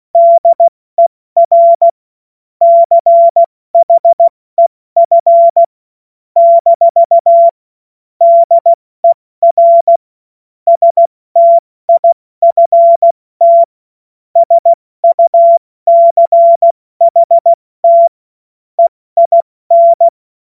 Morse code, heard as a keyed tone, about 16 wpm, 680 Hz. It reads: DER CHEF - DER STIFT SUCHT EIN
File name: Morsekurs / Witze / Witz 20. Januar 2026 70 F80